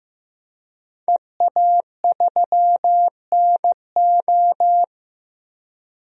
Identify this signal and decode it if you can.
Morse code, heard as a keyed tone, about 15 wpm, 700 Hz. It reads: EA3NO
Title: DXCC-100 Call Signs - 5 Characters - Repeated 2x - 15wpm